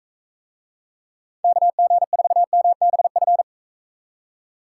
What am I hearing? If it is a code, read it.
Morse code, heard as a keyed tone, about 42 wpm, 700 Hz. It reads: KG4MBF